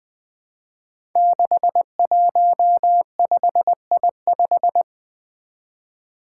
Morse code, heard as a keyed tone, about 20 wpm, 700 Hz. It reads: 615I5
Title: Five Challenging Letter-Number Combinations 20wpm